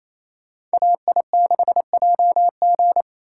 Morse code, heard as a keyed tone, about 36 wpm, 700 Hz. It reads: AI6JG